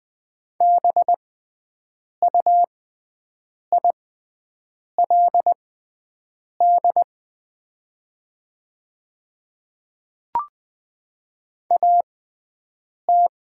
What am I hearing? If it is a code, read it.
Morse code, heard as a keyed tone, about 20 wpm, spaced as for 7 wpm, 700 Hz. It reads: BUILD AT
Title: 6x Character Spacing - Top 200 Words - 20wpm